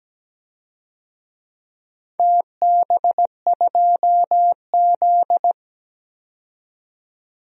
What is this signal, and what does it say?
Morse code, heard as a keyed tone, about 17 wpm, 700 Hz. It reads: TB2Z
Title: DXCC-100 Call Signs - 4 Characters - 17wpm